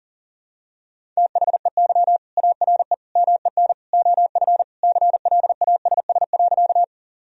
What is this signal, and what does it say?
Morse code, heard as a keyed tone, about 40 wpm, 700 Hz. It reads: THEY ARE MEN OF CLASS.
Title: Sentences from Top 1000 Words - Review Part 2 - 40wpm